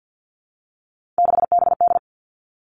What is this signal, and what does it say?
Morse code, heard as a keyed tone, about 50 wpm, 700 Hz. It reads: 6BD
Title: Three Challenging Letter-Number Combinations 50wpm